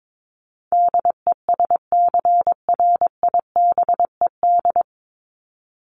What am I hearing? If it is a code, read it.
Morse code, heard as a keyed tone, about 22 wpm, 700 Hz. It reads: DESCRIBED
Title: ED Suffix - Top 100 Words with Base Words 22wpm